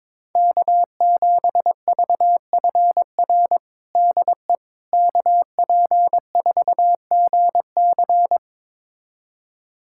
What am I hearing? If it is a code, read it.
Morse code, heard as a keyed tone, about 22 wpm, 700 Hz. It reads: K7VFR DE KP4GC
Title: USA Callsign DE Callsign - 5 Characters - 22wpm